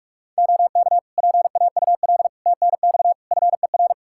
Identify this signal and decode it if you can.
Morse code, heard as a keyed tone, about 46 wpm, 700 Hz. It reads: OK PAUL TNX FER
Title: K1USN SST QSO Practice 45wpm